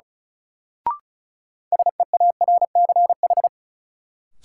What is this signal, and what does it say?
Morse code, heard as a keyed tone, about 34 wpm, 700 Hz. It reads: SEARCH
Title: Sentences from Top 800 Words - New Word Drill - 35wpm